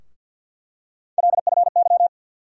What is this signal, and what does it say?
Morse code, heard as a keyed tone, about 50 wpm, 700 Hz. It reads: LFY
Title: USA Call Sign Postfixes - 3 Letters 50wpm